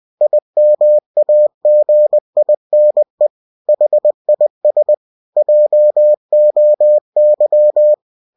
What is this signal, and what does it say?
Morse code, heard as a keyed tone, about 20 wpm, 600 Hz. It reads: IMAGINE HIS JOY
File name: cw-books / winnie-the-pooh / winnie-20wpm-page-0101